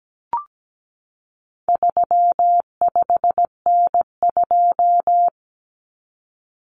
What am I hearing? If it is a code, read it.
Morse code, heard as a keyed tone, about 17 wpm, 700 Hz. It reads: 35N2
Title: Common Cut Numbers - 4-Digit 17wpm